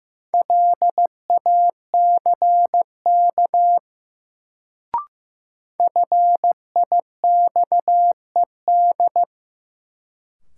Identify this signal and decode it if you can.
Morse code, heard as a keyed tone, about 15 wpm, 700 Hz. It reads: LACK FIXED
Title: Sentences from Top 1100 Words - New Word Drill - 15wpm